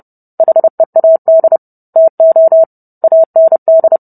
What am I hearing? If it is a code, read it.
Morse code, heard as a keyed tone, about 30 wpm, 650 Hz. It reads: HEAD TO AND